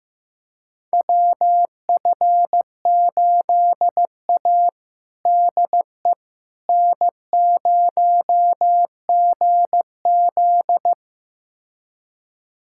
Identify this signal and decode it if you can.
Morse code, heard as a keyed tone, about 15 wpm, 700 Hz. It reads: WF8A DE N0GZ
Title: USA Callsign DE Callsign - 4 Characters - 15wpm